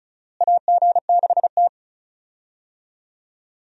Morse code, heard as a keyed tone, about 36 wpm, 700 Hz. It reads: AG6T
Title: USA Call Signs - 4 Characters - Repeated 2x - 35wpm